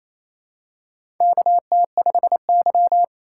Morse code, heard as a keyed tone, about 27 wpm, 700 Hz. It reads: KT5Y
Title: USA Call Signs - 4 Characters - 2x Character Spacing - 28wpm